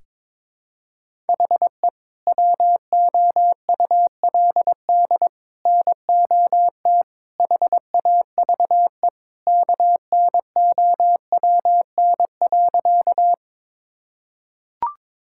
Morse code, heard as a keyed tone, about 22 wpm, 700 Hz. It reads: HE WOULD NOT HAVE KNOWN.
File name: practice / 22wpm / Sentences with 3-Word Phrases - 22wpm